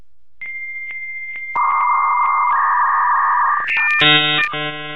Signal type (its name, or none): dial-up modem handshake